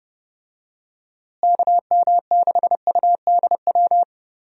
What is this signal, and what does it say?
Morse code, heard as a keyed tone, about 30 wpm, 700 Hz. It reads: KM6UDW
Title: USA Call Signs - 6 Characters - Repeated 3x - 30wpm